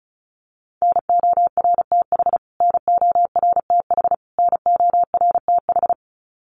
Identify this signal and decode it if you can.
Morse code, heard as a keyed tone, about 34 wpm, 700 Hz. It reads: NORTH NORTH NORTH
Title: Top 500 Words Repeated 35wpm